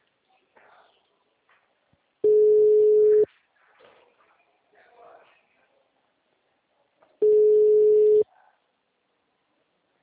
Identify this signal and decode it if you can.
Ringback tone (Europe, 425 Hz), with speech.